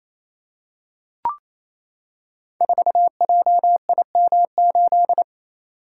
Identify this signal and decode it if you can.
Morse code, heard as a keyed tone, about 29 wpm, 700 Hz. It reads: 4JIM8
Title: Code Groups - 5 Letter:Numbers - 28wpm